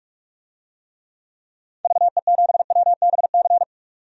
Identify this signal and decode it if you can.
Morse code, heard as a keyed tone, about 46 wpm, 700 Hz. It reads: VE7WBC
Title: DXCC-100 Call Signs - 6 Characters - 45wpm